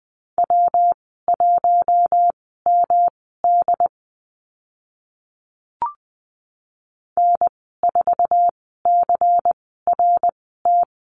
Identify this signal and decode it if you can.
Morse code, heard as a keyed tone, about 20 wpm, spaced as for 14 wpm, 700 Hz. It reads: W1MD N4CRT
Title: 140 Lesson 23g - Letter f - Call Signs 20wpm